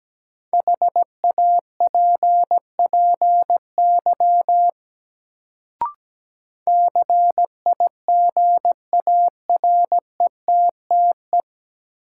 Morse code, heard as a keyed tone, about 17 wpm, 700 Hz. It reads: HAPPY CIGARETTE
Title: Sentences from Top 1000 Words - New Word Drill - 17wpm